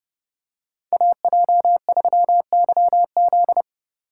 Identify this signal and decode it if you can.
Morse code, heard as a keyed tone, about 29 wpm, 700 Hz. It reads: AJ3YZ